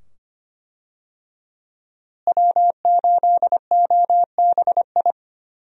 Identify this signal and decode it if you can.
Morse code, heard as a keyed tone, about 25 wpm, 700 Hz. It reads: W8OBI
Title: USA Call Signs - 5 Characters - Repeated 2x - 25wpm